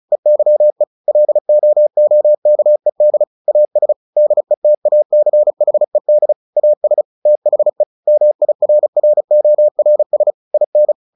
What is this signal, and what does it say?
Morse code, heard as a keyed tone, about 34 wpm, 600 Hz. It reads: EYE LOOKED AS DETACHED AS THE MIRRORS IN
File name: the old man and the sea 2 35wpm